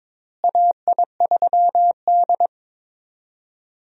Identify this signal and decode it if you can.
Morse code, heard as a keyed tone, about 22 wpm, 700 Hz. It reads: AI3D
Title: USA Call Signs - 4 Characters - Repeated 3x - 22wpm